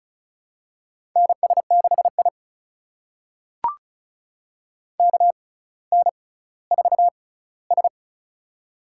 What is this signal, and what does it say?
Morse code, heard as a keyed tone, about 36 wpm, spaced as for 18 wpm, 700 Hz. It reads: NS6I KN4S
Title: USA Call Signs - 4 Characters - 6x Character Spacing - 35wpm